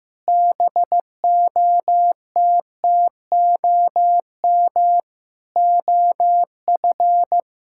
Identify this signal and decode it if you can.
Morse code, heard as a keyed tone, about 15 wpm, 700 Hz. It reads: BOTTOM OF
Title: Sentences from Top 1000 Words - Review Part 6 - 15wpm